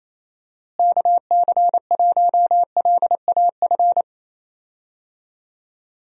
Morse code, heard as a keyed tone, about 28 wpm, 700 Hz. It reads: KC1LAF
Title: USA Call Signs - 6 Characters - Repeated 3x - 28wpm